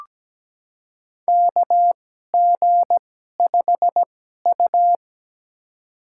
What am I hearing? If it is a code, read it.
Morse code, heard as a keyed tone, about 17 wpm, spaced as for 12 wpm, 700 Hz. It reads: KG5U
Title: USA Call Signs - 4 Characters - 2x Character Spacing - 17wpm